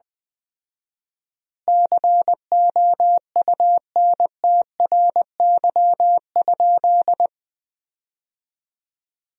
Morse code, heard as a keyed tone, about 20 wpm, 700 Hz. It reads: COUNTRY?
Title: 4x Word-Spacing - Sentences from Top 300 Words - 20wpm